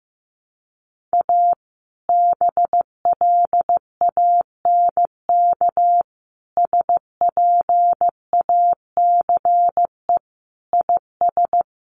Morse code, heard as a keyed tone, about 14 wpm, 700 Hz. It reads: A BLANK SPACE IS
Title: Sentences from Top 1500 Words 15wpm